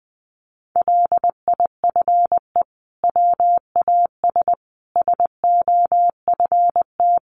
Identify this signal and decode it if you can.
Morse code, heard as a keyed tone, about 20 wpm, 700 Hz. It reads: LIFE WAS SOFT
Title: Sentences from Top 1500 Words 20wpm